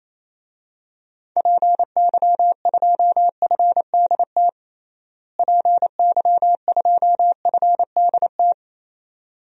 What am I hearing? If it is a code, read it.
Morse code, heard as a keyed tone, about 28 wpm, 700 Hz. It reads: PY2FDT PY2FDT
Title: DXCC-100 Call Signs - 6 Characters - Repeated 2x - 28wpm